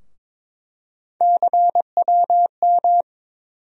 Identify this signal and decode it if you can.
Morse code, heard as a keyed tone, about 22 wpm, 700 Hz. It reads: CWM
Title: USA Call Sign Postfixes - 3 Letters 22wpm